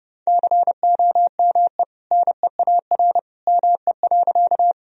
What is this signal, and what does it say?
Morse code, heard as a keyed tone, about 30 wpm, 700 Hz. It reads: COME NEAR ME.
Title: Sentences from Top 1000 Words - Review Part 1 - 30wpm